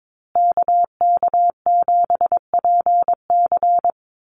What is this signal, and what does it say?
Morse code, heard as a keyed tone, about 22 wpm, 690 Hz. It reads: KK7PC
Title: USA Call Signs Repeated 3x 22wpm